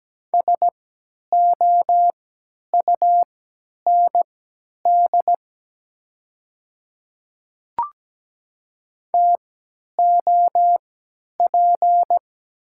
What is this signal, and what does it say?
Morse code, heard as a keyed tone, about 17 wpm, spaced as for 10 wpm, 700 Hz. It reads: SOUND TOP
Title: 3x Character Spacing - Top 500 Words - 17wpm